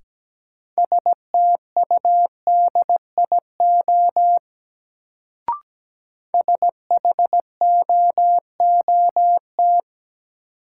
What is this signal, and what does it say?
Morse code, heard as a keyed tone, about 17 wpm, 700 Hz. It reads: STUDIO SHOOT
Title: Sentences from Top 1200 Words - New Word Drill - 17wpm